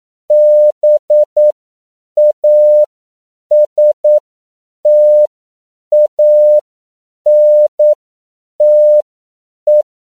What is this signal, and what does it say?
Morse code, heard as a keyed tone, about 9 wpm, spaced as for 7 wpm, 600 Hz. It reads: BASTANTE